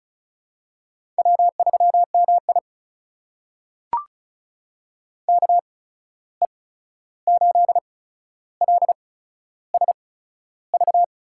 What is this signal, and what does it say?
Morse code, heard as a keyed tone, about 34 wpm, spaced as for 13 wpm, 700 Hz. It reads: W3MI KE8LSV
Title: USA Call Signs - 8x Character Spacing - 35wpm